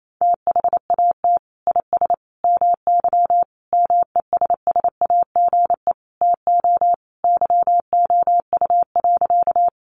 Morse code, heard as a keyed tone, about 28 wpm, 700 Hz. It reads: THAT IS MY MESSAGE TO YOU.
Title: Sentences from Top 1100 Words 28wpm